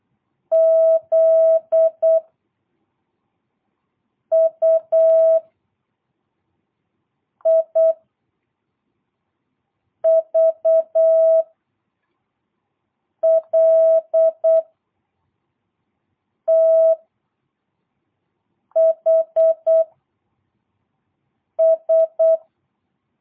Morse code, heard as a keyed tone, about 8 wpm, 660 Hz. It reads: Z U I V L T H S
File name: jh8cbh / 2022cw